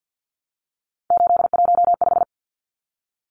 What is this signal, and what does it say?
Morse code, heard as a keyed tone, about 50 wpm, 700 Hz. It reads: 815
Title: Numbers - 3-Digit 50wpm